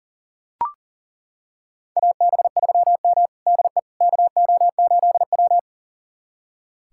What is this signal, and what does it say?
Morse code, heard as a keyed tone, about 40 wpm, 700 Hz. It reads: AB3M DE KO8W
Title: USA Callsign DE Callsign - 4 Characters - 40wpm